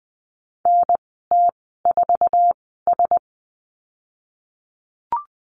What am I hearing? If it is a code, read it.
Morse code, heard as a keyed tone, about 20 wpm, spaced as for 15 wpm, 700 Hz. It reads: NT4S
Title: 055 Lesson 10g - Letter r - Call Signs 20wpm